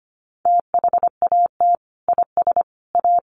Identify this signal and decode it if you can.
Morse code, heard as a keyed tone, about 25 wpm, 700 Hz. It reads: THAT IS A
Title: Sentences from Top 1500 Words 25wpm